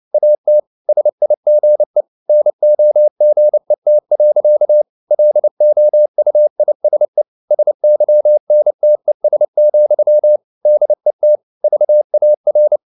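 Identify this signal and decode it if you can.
Morse code, heard as a keyed tone, about 29 wpm, 600 Hz. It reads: AT SIGE NOGET. LOUISE SYNTES, DET VAR
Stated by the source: Fast (QRQ) Morse at 29 wpm, from CW med Gnister.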